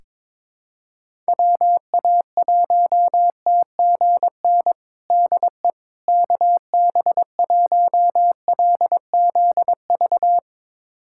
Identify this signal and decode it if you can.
Morse code, heard as a keyed tone, about 22 wpm, 700 Hz. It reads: WA1TGN DE KB1LZV
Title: USA Callsign DE Callsign - 6 Characters - 22wpm